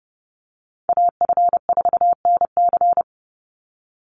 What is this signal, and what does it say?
Morse code, heard as a keyed tone, about 30 wpm, 700 Hz. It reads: AF4NC
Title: Call Signs - 30wpm